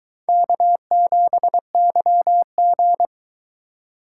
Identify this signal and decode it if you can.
Morse code, heard as a keyed tone, about 22 wpm, 700 Hz. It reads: K7YG